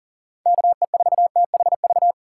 Morse code, heard as a keyed tone, about 40 wpm, 700 Hz. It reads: KE4THV
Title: USA Call Signs - 6 Characters - 6x Character Spacing - 40wpm